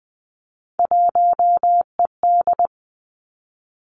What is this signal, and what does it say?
Morse code, heard as a keyed tone, about 20 wpm, 700 Hz. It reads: 1ED
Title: Three Challenging Letter-Number Combinations 20wpm